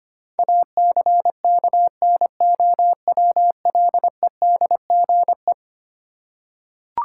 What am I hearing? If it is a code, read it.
Morse code, heard as a keyed tone, about 25 wpm, 700 Hz. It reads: ACKNOWLEDGE
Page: morse-code-ninja-practice-25wpm